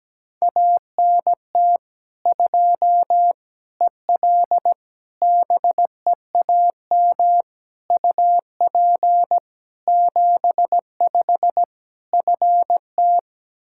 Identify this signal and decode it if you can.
Morse code, heard as a keyed tone, about 17 wpm, 700 Hz. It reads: ANT 2 EL BEAM UP 75 FT
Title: Practice QSO - Exchange 2 - Info - 17wpm